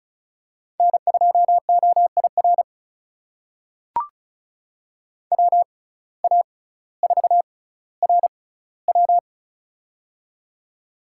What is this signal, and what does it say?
Morse code, heard as a keyed tone, about 36 wpm, spaced as for 18 wpm, 700 Hz. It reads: N2OIR WA4RW
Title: USA Call Signs - 5 Characters - 6x Character Spacing - 35wpm